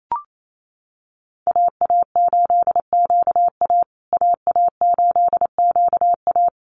Morse code, heard as a keyed tone, about 28 wpm, 700 Hz. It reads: AA8QA AA8QA
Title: USA Call Signs Repeated 3x 28wpm